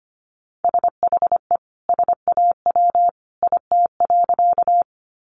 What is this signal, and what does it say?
Morse code, heard as a keyed tone, about 25 wpm, 700 Hz. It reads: SHE SAW IT.
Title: Sentences from Top 300 Words 25wpm